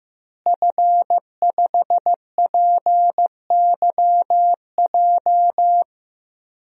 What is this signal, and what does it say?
Morse code, heard as a keyed tone, about 15 wpm, 700 Hz. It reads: F5PYJ
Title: DXCC-100 Call Signs - 5 Characters - Repeated 2x - 15wpm